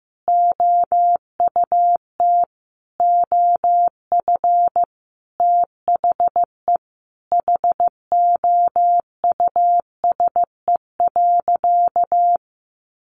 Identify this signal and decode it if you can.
Morse code, heard as a keyed tone, about 15 wpm, 700 Hz. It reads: OUT OF THE HOUSE.
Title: Sentences from Top 1500 Words 15wpm